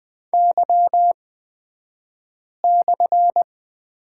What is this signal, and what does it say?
Morse code, heard as a keyed tone, about 20 wpm, spaced as for 5 wpm, 700 Hz. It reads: Y/
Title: 242 Lesson 39d - Letter z - Two Characters 20wpm